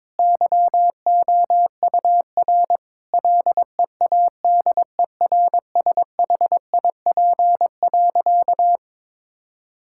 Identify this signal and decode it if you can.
Morse code, heard as a keyed tone, about 22 wpm, 700 Hz. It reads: YOUR LEADERSHIP.
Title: Sentences from Top 1600 Words 22wpm